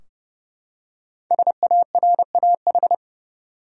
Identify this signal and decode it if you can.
Morse code, heard as a keyed tone, about 30 wpm, 700 Hz. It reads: SARAH
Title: USA Common Names - Max 5 Letters - 30wpm